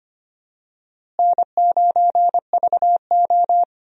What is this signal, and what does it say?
Morse code, heard as a keyed tone, about 25 wpm, 700 Hz. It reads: N9VO
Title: USA Call Signs - 4 Characters - Repeated 2x - 25wpm